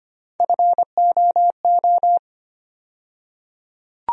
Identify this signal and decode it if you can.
Morse code, heard as a keyed tone, about 25 wpm, 700 Hz. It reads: FOO
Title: USA Call Sign Postfixes - 3 Letters 25wpm